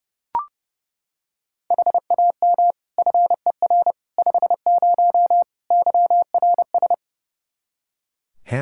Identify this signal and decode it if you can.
Morse code, heard as a keyed tone, about 30 wpm, 700 Hz. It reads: HAM FER 50 YRS
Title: Practice QSO - Exchange 3 - More Info - 30wpm